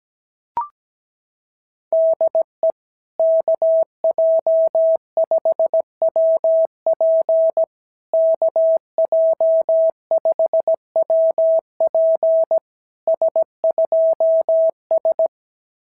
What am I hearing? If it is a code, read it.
Morse code, heard as a keyed tone, about 17 wpm, 650 Hz. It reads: DE KJ5WP KJ5WP S2S
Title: SOTA QSO Practice 17wpm